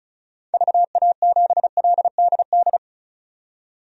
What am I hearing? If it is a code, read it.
Morse code, heard as a keyed tone, about 34 wpm, 700 Hz. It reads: VA7LDD
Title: DXCC-100 Call Signs - 6 Characters - 35wpm